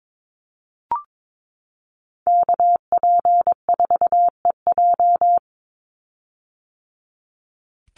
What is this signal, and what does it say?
Morse code, heard as a keyed tone, about 22 wpm, 700 Hz. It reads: KP4EJ
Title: Call Signs - 22wpm